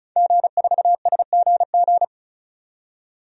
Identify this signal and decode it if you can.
Morse code, heard as a keyed tone, about 36 wpm, 700 Hz. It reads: G4SGG
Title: DXCC-100 Call Signs - 5 Characters - 35wpm